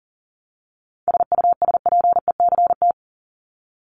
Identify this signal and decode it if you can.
Morse code, heard as a keyed tone, about 40 wpm, 700 Hz. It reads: SUSPECT
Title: Top 1751-2000 Words 40wpm